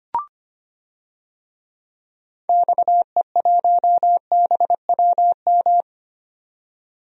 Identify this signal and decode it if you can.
Morse code, heard as a keyed tone, about 25 wpm, 700 Hz. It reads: XE1BWM
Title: DXCC-100 Call Signs - 6 Characters - 25wpm